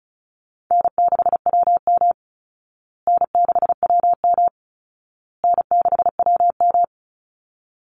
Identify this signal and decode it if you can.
Morse code, heard as a keyed tone, about 34 wpm, 700 Hz. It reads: N6WM N6WM N6WM
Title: Call Signs Repeated 2x - 35wpm